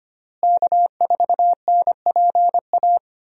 Morse code, heard as a keyed tone, about 25 wpm, 700 Hz. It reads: K4NPA
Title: USA Call Signs - 5 Characters - Repeated 3x - 25wpm